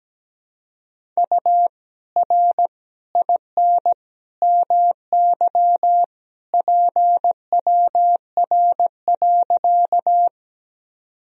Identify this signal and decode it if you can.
Morse code, heard as a keyed tone, about 17 wpm, 700 Hz. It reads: U R IN MY PWR.
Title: Sentences from Top 500 Words and Abbreviations 17wpm